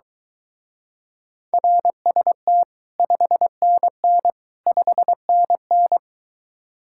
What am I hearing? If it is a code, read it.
Morse code, heard as a keyed tone, about 23 wpm, 700 Hz. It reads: RST 5NN 5NN